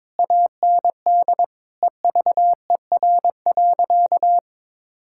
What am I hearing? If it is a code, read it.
Morse code, heard as a keyed tone, about 22 wpm, 700 Hz. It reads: AND EVER.
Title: Sentences from Top 1600 Words 22wpm